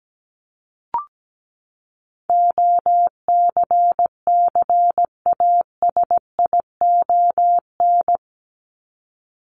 Morse code, heard as a keyed tone, about 17 wpm, 700 Hz. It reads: OCCASION
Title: Top 1501-1750 Words 17wpm